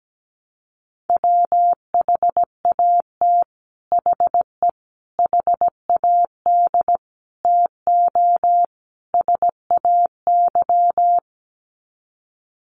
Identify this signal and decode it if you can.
Morse code, heard as a keyed tone, about 17 wpm, 700 Hz. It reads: WHAT HE HAD TO SAY
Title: Sets of 5 Words Encore2 17wpm